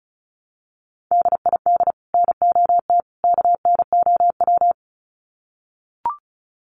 Morse code, heard as a keyed tone, about 34 wpm, 700 Hz. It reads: DID NOT KNOW
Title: Sets of 3 Words 35wpm